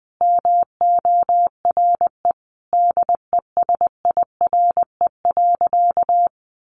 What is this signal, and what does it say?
Morse code, heard as a keyed tone, about 20 wpm, 700 Hz. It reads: MORE DESIRE.
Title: Sentences from Top 1100 Words 20wpm